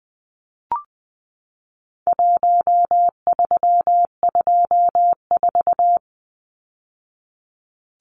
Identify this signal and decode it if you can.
Morse code, heard as a keyed tone, about 20 wpm, 700 Hz. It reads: 1324
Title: Numbers - Contest Style - 4-Digit 20wpm